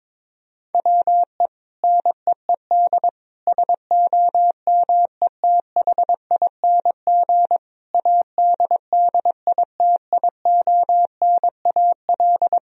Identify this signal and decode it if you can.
Morse code, heard as a keyed tone, about 22 wpm, 700 Hz. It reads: WE NEED SOMETHING ADDITIONAL
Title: Sentences from Top 1000 Words - Review Part 4 - 22wpm